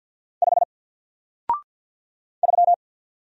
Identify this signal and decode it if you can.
Morse code, heard as a keyed tone, about 48 wpm, 700 Hz. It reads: 5 3
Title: Single Letter-Number Alt 50wpm